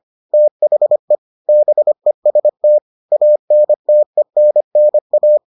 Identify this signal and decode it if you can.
Morse code, heard as a keyed tone, about 25 wpm, 600 Hz. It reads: THE BEST ANTENNA